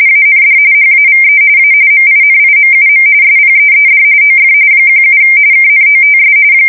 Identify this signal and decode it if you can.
RTTY, about 45.45 baud. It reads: AND HAVE A MINIMUM SPEED OF 17 KNOTS
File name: fsk170 / battleships